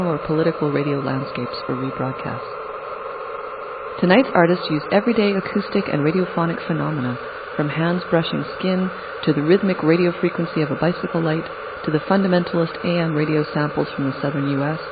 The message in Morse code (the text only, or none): none